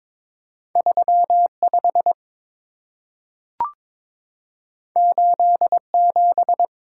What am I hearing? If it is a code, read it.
Morse code, heard as a keyed tone, about 22 wpm, 700 Hz. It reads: 35 87
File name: practice / 22wpm / Two Letter-Number Combinations - 22wpm